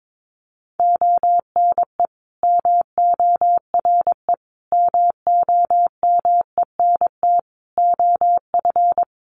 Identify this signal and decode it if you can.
Morse code, heard as a keyed tone, about 22 wpm, 700 Hz. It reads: ONE MORE MOMENT OF
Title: Sentences from Top 1300 Words 22wpm